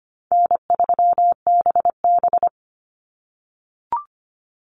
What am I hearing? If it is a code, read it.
Morse code, heard as a keyed tone, about 25 wpm, 700 Hz. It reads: N3BB
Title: Easy Call Signs - 25wpm